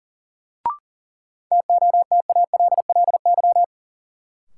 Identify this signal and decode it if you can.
Morse code, heard as a keyed tone, about 40 wpm, 700 Hz. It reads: TOTALLY
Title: Sentences from Top 1600 Words - New Word Drill Alt - 40wpm